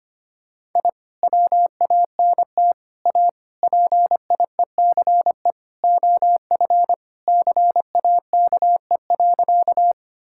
Morse code, heard as a keyed tone, about 25 wpm, 700 Hz. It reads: I WANT A PIECE OF CAKE.
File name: practice / 25wpm / Sentences with 3-Word Phrases - 25wpm